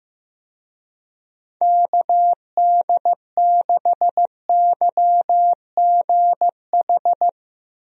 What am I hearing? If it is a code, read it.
Morse code, heard as a keyed tone, about 15 wpm, 700 Hz. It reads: KD6YGH
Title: USA Call Signs - 6 Characters - Repeated 2x - 15wpm